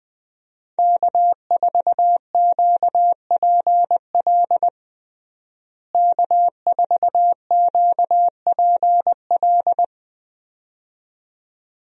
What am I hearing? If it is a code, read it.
Morse code, heard as a keyed tone, about 20 wpm, 700 Hz. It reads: K4QPL K4QPL
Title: USA Call Signs - 5 Characters - Repeated 2x - 20wpm